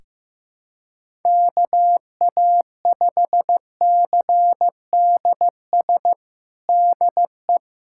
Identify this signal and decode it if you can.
Morse code, heard as a keyed tone, about 16 wpm, 700 Hz. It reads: KA5CDS DE
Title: USA Callsign DE Callsign 15wpm